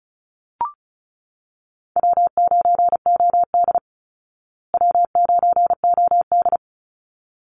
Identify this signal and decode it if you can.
Morse code, heard as a keyed tone, about 34 wpm, 700 Hz. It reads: W9OD W9OD
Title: Easy Call Signs Repeated 2x - 35wpm